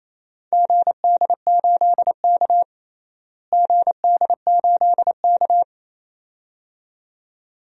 Morse code, heard as a keyed tone, about 28 wpm, 700 Hz. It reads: GD8K GD8K
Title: DXCC-100 Call Signs - 4 Characters - Repeated 2x - 28wpm